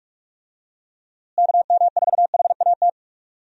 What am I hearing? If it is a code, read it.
Morse code, heard as a keyed tone, about 46 wpm, 700 Hz. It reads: KM4HAT
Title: USA Call Signs - 6 Characters - Repeated 2x - 45wpm